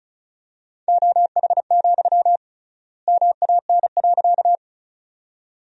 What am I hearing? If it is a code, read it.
Morse code, heard as a keyed tone, about 36 wpm, 700 Hz. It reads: OH, MAN.
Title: 3x Word-Spacing - Sentences from Top 500 Words - 35wpm